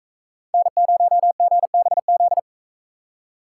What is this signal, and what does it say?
Morse code, heard as a keyed tone, about 41 wpm, 700 Hz. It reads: N0GBZ